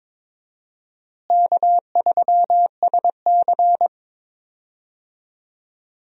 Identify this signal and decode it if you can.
Morse code, heard as a keyed tone, about 22 wpm, 700 Hz. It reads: K3SC
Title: USA Call Signs - 4 Characters - Repeated 2x - 22wpm